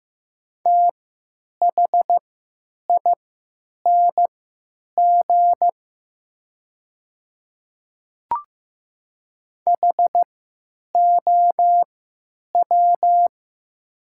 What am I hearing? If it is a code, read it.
Morse code, heard as a keyed tone, about 15 wpm, spaced as for 9 wpm, 700 Hz. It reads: THING HOW
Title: 3x Character Spacing - Top 200 Words - 15wpm